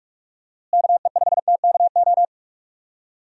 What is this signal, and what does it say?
Morse code, heard as a keyed tone, about 41 wpm, 700 Hz. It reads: KE5TKO
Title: USA Call Signs - 6 Characters - Repeated 2x - 45wpm